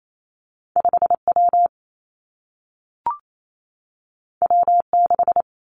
Morse code, heard as a keyed tone, about 28 wpm, 700 Hz. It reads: 5 W W 6
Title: Two Challenging Letter-Number Combinations 28wpm